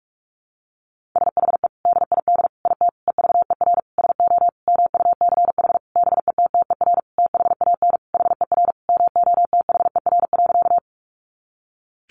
Morse code, heard as a keyed tone, about 46 wpm, 700 Hz. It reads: SHE DID IT EVER SO MUCH BETTER THAN HER MOTHER.
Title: Sentences from Top 400 Words 45wpm